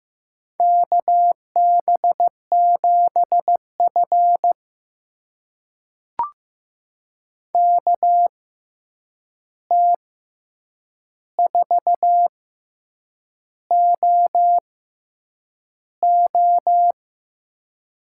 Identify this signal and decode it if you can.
Morse code, heard as a keyed tone, about 15 wpm, spaced as for 7 wpm, 700 Hz. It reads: KB7F KT4OO
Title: USA Call Signs - 6x Character Spacing - 15wpm